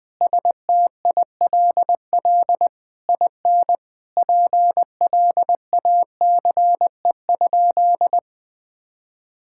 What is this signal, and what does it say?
Morse code, heard as a keyed tone, about 20 wpm, 700 Hz. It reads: STILL IN PLACE?
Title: Sentences from Top 1000 Words - Review Part 6 - 20wpm